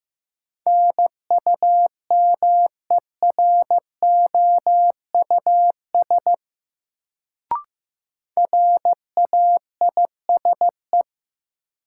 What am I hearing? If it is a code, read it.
Morse code, heard as a keyed tone, about 15 wpm, 700 Hz. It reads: NUMEROUS RAISE
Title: Sentences from Top 1400 Words - New Word Drill Alt - 15wpm